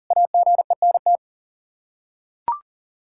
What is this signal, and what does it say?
Morse code, heard as a keyed tone, about 40 wpm, 700 Hz. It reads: AGENT
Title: K1USN SST QSO Practice 40wpm